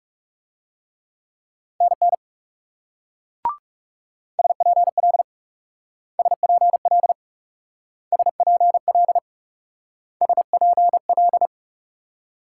Morse code, heard as a keyed tone, about 37 wpm, 700 Hz. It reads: NN SPL SPL SPL SPL
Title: Speed-Racing Multi-letter Phonemes - 30wpm